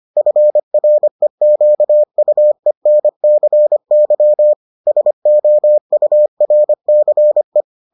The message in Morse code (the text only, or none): FREQUENCY SOURCE